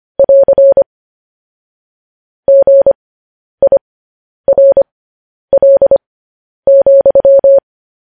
Morse code, heard as a keyed tone, about 25 wpm, spaced as for 10 wpm, 550 Hz. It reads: <AR> GIRL,
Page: News Headlines in Morse Code at 10 WPM